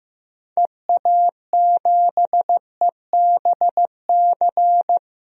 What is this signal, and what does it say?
Morse code, heard as a keyed tone, about 15 wpm, 700 Hz. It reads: EA7EBC
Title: DXCC-100 Call Signs - 6 Characters - 15wpm